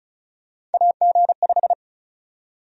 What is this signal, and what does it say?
Morse code, heard as a keyed tone, about 36 wpm, 700 Hz. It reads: AG5
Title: USA Call Sign Prefixes - 35wpm